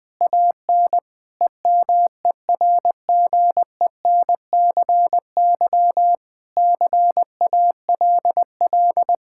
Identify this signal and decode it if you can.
Morse code, heard as a keyed tone, about 20 wpm, 700 Hz. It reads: AN EMERGENCY CALL
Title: Sentences with 3-Word Phrases - 20wpm